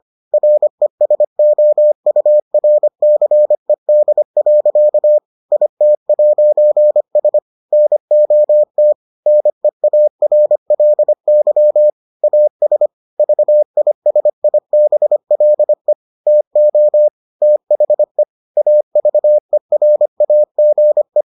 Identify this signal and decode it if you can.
Morse code, heard as a keyed tone, about 25 wpm, 600 Hz. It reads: RESOURCED. IT'S NOT NEARLY AS VISIBLE TO THE AVERAGE